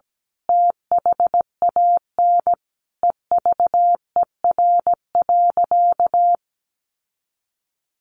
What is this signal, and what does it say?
Morse code, heard as a keyed tone, about 17 wpm, 700 Hz. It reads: THAN EVER.
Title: Sentences from Top 1200 Words 17wpm